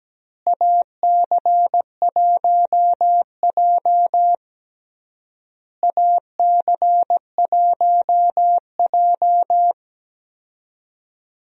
Morse code, heard as a keyed tone, about 17 wpm, 700 Hz. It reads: AC1J AC1J
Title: USA Call Signs - 4 Characters - Repeated 3x - 17wpm